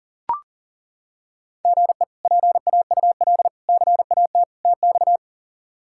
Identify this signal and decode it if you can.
Morse code, heard as a keyed tone, about 40 wpm, 700 Hz. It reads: GE PAUL CAT TX
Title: K1USN SST QSO Practice - 40wpm